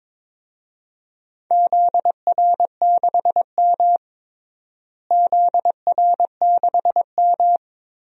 Morse code, heard as a keyed tone, about 22 wpm, 700 Hz. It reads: ZR6M ZR6M
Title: DXCC-100 Call Signs - 4 Characters - Repeated 2x - 22wpm